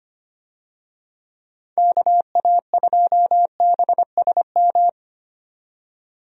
Morse code, heard as a keyed tone, about 25 wpm, 700 Hz. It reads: KA2BSM
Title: USA Call Signs - 6 Characters - 25wpm